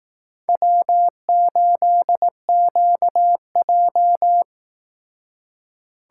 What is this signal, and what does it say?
Morse code, heard as a keyed tone, about 18 wpm, 700 Hz. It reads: W8QJ
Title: Speed Racing - USA 1x2 Call Signs 15wpm